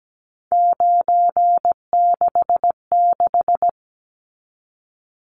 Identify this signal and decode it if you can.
Morse code, heard as a keyed tone, about 17 wpm, 700 Hz. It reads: 966
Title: Numbers - 3-Digit 17wpm